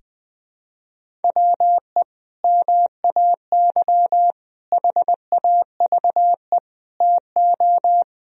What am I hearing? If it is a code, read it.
Morse code, heard as a keyed tone, about 21 wpm, 700 Hz. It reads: WE MAY HAVE TO
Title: Sentences from Top 1000 Words - Review Part 1 - 20wpm